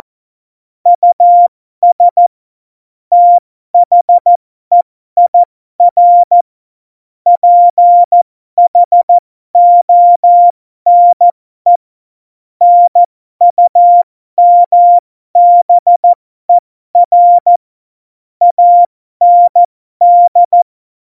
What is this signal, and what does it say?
Morse code, heard as a keyed tone, about 14 wpm, spaced as for 12 wpm, 700 Hz. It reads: US THEIR PHONE NUMBER AND
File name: Morsekurs / Witze